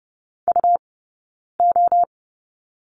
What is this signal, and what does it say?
Morse code, heard as a keyed tone, about 30 wpm, spaced as for 9 wpm, 690 Hz. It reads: UO
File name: Lesson 19d - Letter w - Two Characters 30wpm